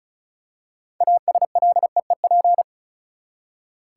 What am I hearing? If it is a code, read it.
Morse code, heard as a keyed tone, about 36 wpm, 700 Hz. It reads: ASLEEP
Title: Six Letter Words - 35wpm